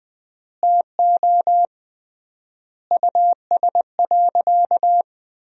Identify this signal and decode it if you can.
Morse code, heard as a keyed tone, about 20 wpm, 700 Hz. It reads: TO US.
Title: 3x Word-Spacing - Sentences from Top 500 Words - 20wpm